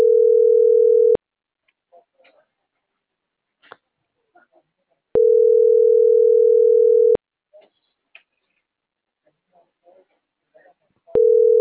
Ringback tone (North America), 440 + 480 Hz.